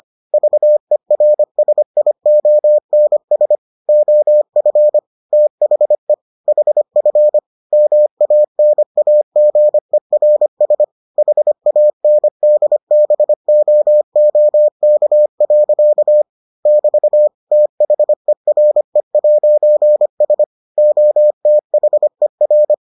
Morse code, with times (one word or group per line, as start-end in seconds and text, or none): 0.3-3.6 VERSIONS
3.9-5.0 OF
5.3-6.1 THE
6.5-7.4 HF
7.7-10.8 MANAGERS
11.2-16.2 HANDBOOK.
16.7-17.3 <BT>
17.5-20.4 THERE'S
20.8-22.7 OTHER